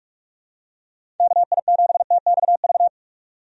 Morse code, heard as a keyed tone, about 46 wpm, 700 Hz. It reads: KI7TXV
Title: USA Call Signs - 6 Characters - 45wpm